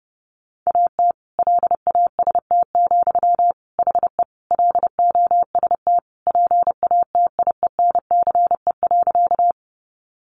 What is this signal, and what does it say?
Morse code, heard as a keyed tone, about 30 wpm, 700 Hz. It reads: AT LAST, HE LOST PATIENCE.
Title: Sentences from Top 1400 Words 30wpm